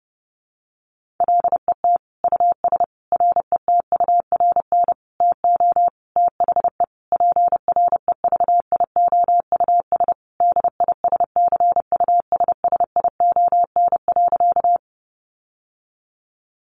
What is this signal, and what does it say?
Morse code, heard as a keyed tone, about 30 wpm, 700 Hz. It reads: LET US RETURN TO THE PREVIOUS DISCUSSION.
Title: Sentences from Top 1500 Words 30wpm